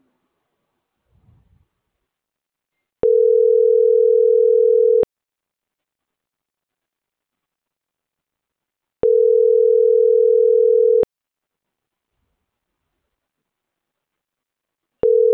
Ringback tone (North America), 440 + 480 Hz.